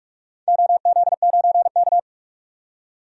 Morse code, heard as a keyed tone, about 47 wpm, 700 Hz. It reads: OZ9K
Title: DXCC-100 Call Signs - 4 Characters - 45wpm